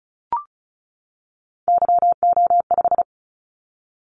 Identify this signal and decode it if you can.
Morse code, heard as a keyed tone, about 36 wpm, 700 Hz. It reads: YO5
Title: Callsign Prefixes for Top 25 DXCC - 35wpm